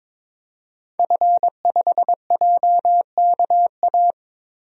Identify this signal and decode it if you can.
Morse code, heard as a keyed tone, about 22 wpm, 700 Hz. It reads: F5JKA
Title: DXCC-100 Call Signs - 5 Characters - 22wpm